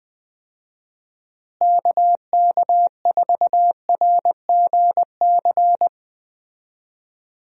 Morse code, heard as a keyed tone, about 20 wpm, 700 Hz. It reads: KK4RGC
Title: USA Call Signs - 6 Characters - Repeated 3x - 20wpm